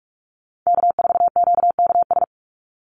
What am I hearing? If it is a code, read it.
Morse code, heard as a keyed tone, about 44 wpm, 700 Hz. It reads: K4QKS
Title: USA Call Signs 45wpm